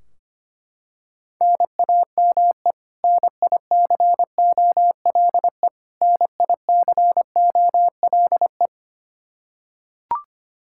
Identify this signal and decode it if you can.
Morse code, heard as a keyed tone, about 25 wpm, 700 Hz. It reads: NAME NICOLE NICOLE
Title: Practice QSO - Exchange 1 - Beginning - 25wpm